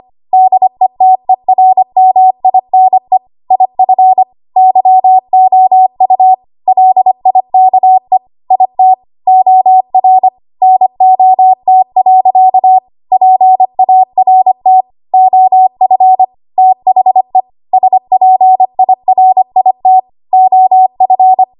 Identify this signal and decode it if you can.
Morse code, heard as a keyed tone, about 25 wpm, 750 Hz. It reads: DETERMINE IF YOU LIKE IT OR NOT. PART OF THE SPIRIT OF
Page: 25 WPM Code Archive